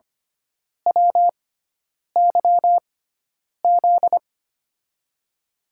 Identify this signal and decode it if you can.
Morse code, heard as a keyed tone, about 25 wpm, spaced as for 9 wpm, 700 Hz. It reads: WYZ
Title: USA Call Signs - 6 Characters - 6x Character Spacing - 25wpm